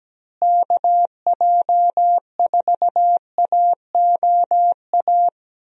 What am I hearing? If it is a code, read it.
Morse code, heard as a keyed tone, about 17 wpm, 700 Hz. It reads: KJ4AOA
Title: USA Call Signs - 6 Characters - Repeated 2x - 17wpm